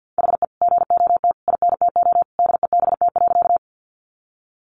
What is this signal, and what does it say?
Morse code, heard as a keyed tone, about 48 wpm, 700 Hz. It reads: HE GOT INTO DEBT.
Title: Sentences from Top 1300 Words 50wpm